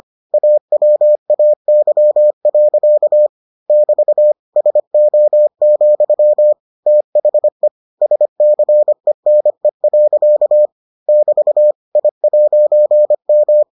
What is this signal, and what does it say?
Morse code, heard as a keyed tone, about 25 wpm, 600 Hz. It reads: AWAY. <BT> SO, THE SCENE. <BT> I'M